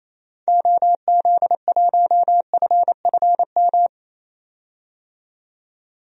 Morse code, heard as a keyed tone, about 28 wpm, 700 Hz. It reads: OZ1FFM
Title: DXCC-100 Call Signs - 6 Characters - 28wpm